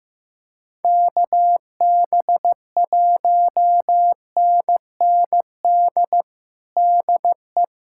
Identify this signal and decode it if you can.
Morse code, heard as a keyed tone, about 15 wpm, 700 Hz. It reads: KB1NND DE
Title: USA Callsign DE Callsign - 6 Characters - 15wpm